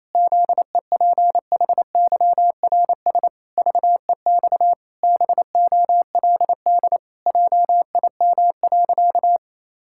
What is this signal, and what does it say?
Morse code, heard as a keyed tone, about 28 wpm, 700 Hz. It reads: ZEPHYRS VEX BOLD JIM.
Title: Pangram Challenge v2 - 28wpm